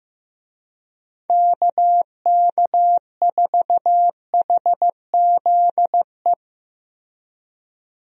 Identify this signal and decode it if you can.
Morse code, heard as a keyed tone, about 15 wpm, 700 Hz. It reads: KK4HZE
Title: USA Call Signs - 6 Characters - Repeated 2x - 15wpm